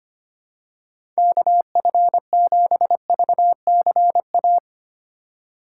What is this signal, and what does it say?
Morse code, heard as a keyed tone, about 25 wpm, 700 Hz. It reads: KF7VCA
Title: USA Call Signs - 6 Characters - 2x Character Spacing - 25wpm